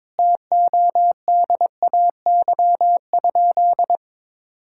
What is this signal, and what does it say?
Morse code, heard as a keyed tone, about 22 wpm, 700 Hz. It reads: TODAY?
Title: Sentences from Top 1000 Words - Review Part 6 - 22wpm